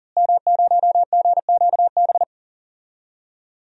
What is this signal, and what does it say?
Morse code, heard as a keyed tone, about 37 wpm, 700 Hz. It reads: M0GQB
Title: DXCC-100 Call Signs - 5 Characters - Repeated 3x - 40wpm